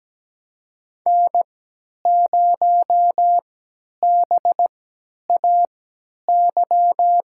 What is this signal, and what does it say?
Morse code, heard as a keyed tone, about 17 wpm, spaced as for 10 wpm, 700 Hz. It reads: N0BAY
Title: USA Call Signs - 5 Characters - 3x Character Spacing - 17wpm